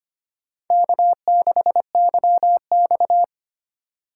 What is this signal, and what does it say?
Morse code, heard as a keyed tone, about 29 wpm, 700 Hz. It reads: K6YX
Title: Speed Racing - USA Call Signs 25wpm